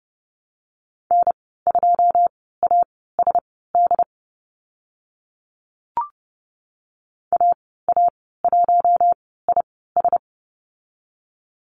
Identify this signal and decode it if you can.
Morse code, heard as a keyed tone, about 30 wpm, spaced as for 17 wpm, 700 Hz. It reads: N2ASD AA1IS
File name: Lesson 14g - Number 2 - Call Signs 30wpm